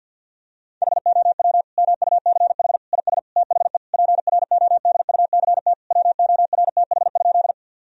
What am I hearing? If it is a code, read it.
Morse code, heard as a keyed tone, about 50 wpm, 700 Hz. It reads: HOW MUCH IS THE PRODUCT WORTH?
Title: Sentences from Top 1000 Words - Review Part 2 - 50wpm